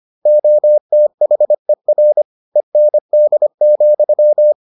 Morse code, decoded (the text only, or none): OTHER END,